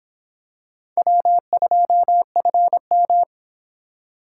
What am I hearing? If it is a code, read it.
Morse code, heard as a keyed tone, about 26 wpm, 700 Hz. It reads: W2FM